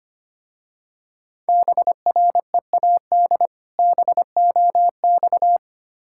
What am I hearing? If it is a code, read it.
Morse code, heard as a keyed tone, about 25 wpm, 700 Hz. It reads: BREAD BOX
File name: practice / 25wpm / Difficult Words - Speed-Racing with Hints - 25wpm